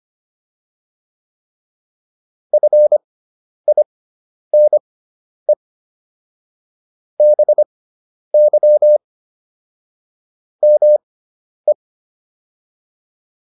Morse code, heard as a keyed tone, about 25 wpm, spaced as for 10 wpm, 600 Hz. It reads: FINE BY ME